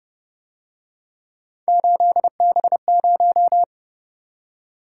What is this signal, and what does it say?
Morse code, heard as a keyed tone, about 29 wpm, 700 Hz. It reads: 8B0